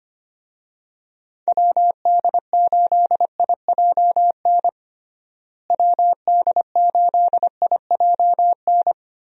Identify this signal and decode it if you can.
Morse code, heard as a keyed tone, about 25 wpm, 700 Hz. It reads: WD8IJN WD8IJN
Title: USA Call Signs - 6 Characters - Repeated 2x - 25wpm